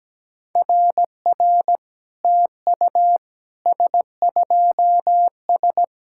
Morse code, heard as a keyed tone, about 17 wpm, 700 Hz. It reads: RR TU S2S
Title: SOTA QSO Practice 17wpm